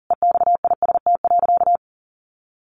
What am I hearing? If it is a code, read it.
Morse code, heard as a keyed tone, about 40 wpm, 700 Hz. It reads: EXIST.
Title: A Sound Of Thunder 40wpm